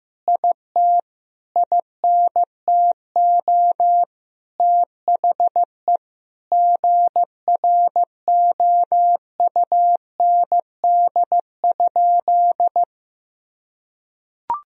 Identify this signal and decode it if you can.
Morse code, heard as a keyed tone, about 16 wpm, 700 Hz. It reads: IT INTO THE GROUND?
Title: Sentences from Top 1000 Words - Review Part 2 - 15wpm